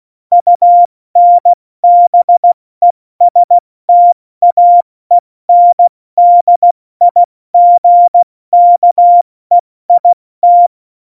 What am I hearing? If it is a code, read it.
Morse code, heard as a keyed tone, about 16 wpm, 700 Hz. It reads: UNBESTAENDIGKEIT
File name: Morsekurs / Witze / Witz 26. Januar 2021, 70 F80